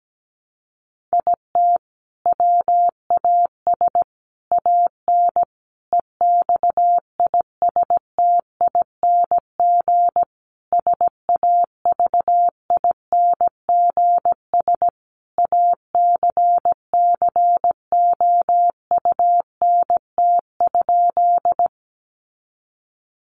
Morse code, heard as a keyed tone, about 17 wpm, 700 Hz. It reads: IT WAS AN EXISTING SAVINGS ACCOUNT?
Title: Sentences from Top 1500 Words 17wpm